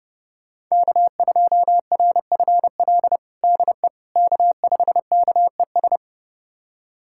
Morse code, heard as a keyed tone, about 32 wpm, 700 Hz. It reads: K2RFL DE K5KES
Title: USA Callsign DE Callsign - 5 Characters - 30wpm